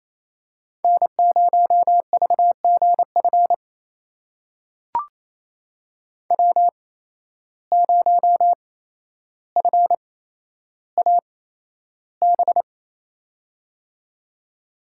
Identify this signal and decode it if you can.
Morse code, heard as a keyed tone, about 28 wpm, spaced as for 12 wpm, 700 Hz. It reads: N0VGF W0FAB
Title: USA Call Signs - 5 Characters - 8x Character Spacing - 28wpm